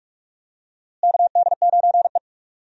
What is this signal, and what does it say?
Morse code, heard as a keyed tone, about 46 wpm, 700 Hz. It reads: KD9E
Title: USA Call Signs - 4 Characters - Repeated 3x - 45wpm